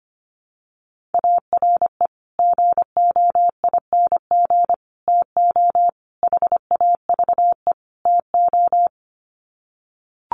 Morse code, heard as a keyed tone, about 25 wpm, 700 Hz. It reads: ARE GOING TO HAVE TO
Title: Sets of 5 Words 25wpm